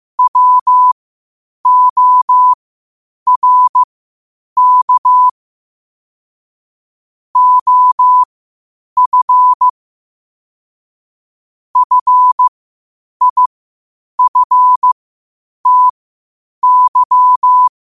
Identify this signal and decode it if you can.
Morse code, heard as a keyed tone, about 15 wpm, spaced as for 8 wpm, 1000 Hz. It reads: WORK OF FIFTY
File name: qotdfiles / 8wpm